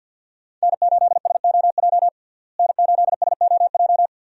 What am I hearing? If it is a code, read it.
Morse code, heard as a keyed tone, about 52 wpm, 700 Hz. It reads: N8SOJ N8SOJ
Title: USA Call Signs - 5 Characters - Repeated 2x - 50wpm